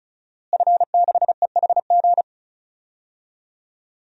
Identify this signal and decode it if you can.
Morse code, heard as a keyed tone, about 36 wpm, 700 Hz. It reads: F6EHG
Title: DXCC-100 Call Signs - 5 Characters - Repeated 3x - 35wpm